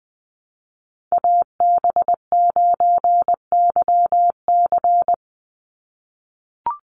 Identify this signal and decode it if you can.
Morse code, heard as a keyed tone, about 20 wpm, 700 Hz. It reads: AB9YC
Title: Call Signs - 20wpm